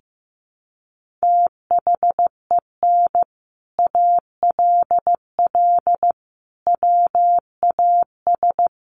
Morse code, heard as a keyed tone, about 15 wpm, 700 Hz. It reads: THEN ALL WAS
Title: Sentences from Top 500 Words 15wpm